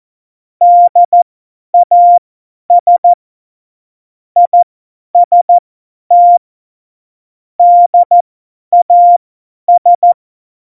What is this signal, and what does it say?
Morse code, heard as a keyed tone, about 14 wpm, spaced as for 10 wpm, 700 Hz. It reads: DAS IST DAS
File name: Morsekurs / Witze / Witz 26. November 2019, 50 F70